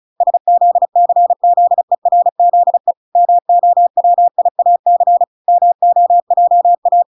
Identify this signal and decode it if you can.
Morse code, heard as a keyed tone, about 34 wpm, 700 Hz. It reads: SZCZERZE MOWIAC MOJA
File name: recordings / 35wpm0000